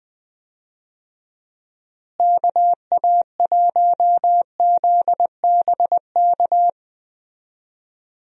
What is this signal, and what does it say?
Morse code, heard as a keyed tone, about 20 wpm, 700 Hz. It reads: KA1ZBK
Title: USA Call Signs - 6 Characters - Repeated 3x - 20wpm